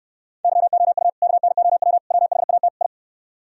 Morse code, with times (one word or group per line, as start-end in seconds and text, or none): 0.4-1.1 YOU
1.2-2.0 KNOW
2.1-2.7 WHAT
2.8-2.9 I